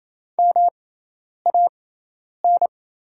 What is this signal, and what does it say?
Morse code, heard as a keyed tone, about 28 wpm, spaced as for 10 wpm, 690 Hz. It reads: MAN
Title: 6x Character Spacing - Top 200 Words - 28wpm